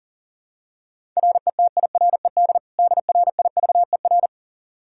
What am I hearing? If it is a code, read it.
Morse code, heard as a keyed tone, about 40 wpm, 700 Hz. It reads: RETIRED DRIVER
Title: K1USN SST QSO Practice 40wpm